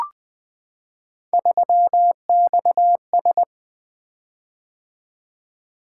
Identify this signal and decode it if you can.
Morse code, heard as a keyed tone, about 20 wpm, 700 Hz. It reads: 3XS
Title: Three Letter-Number Combinations - 20wpm